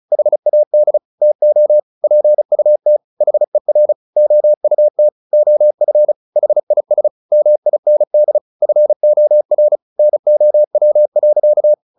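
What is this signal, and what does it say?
Morse code, heard as a keyed tone, about 34 wpm, 600 Hz. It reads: HAD TO PUT HER OUT OF HIS MIND FOR NOW.
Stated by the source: Morse practice at 35 wpm from CW med Gnister, sending Ken Follett's novel Never in English.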